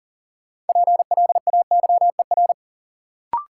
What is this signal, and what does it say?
Morse code, heard as a keyed tone, about 39 wpm, 700 Hz. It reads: PLAYER
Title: Sentences from Top 1000 Words - Review Part 4 - New Word Drill - 40wpm